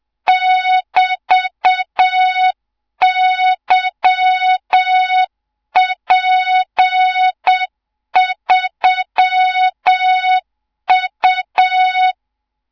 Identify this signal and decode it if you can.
Morse code, heard as a keyed tone, about 7 wpm, 3700 Hz, with noise. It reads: <BT>YP3U
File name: Morze-HA2MN / hang / 24 GYAKORLAT